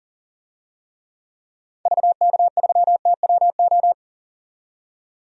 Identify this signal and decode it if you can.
Morse code, heard as a keyed tone, about 40 wpm, 700 Hz. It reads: VK3TWO
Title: DXCC-100 Call Signs - 6 Characters - 40wpm